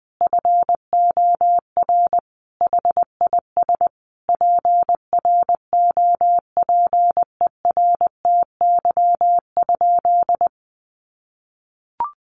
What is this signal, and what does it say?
Morse code, heard as a keyed tone, about 20 wpm, 700 Hz. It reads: FOR HIS PROPERTY?
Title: Sentences from Top 1500 Words 20wpm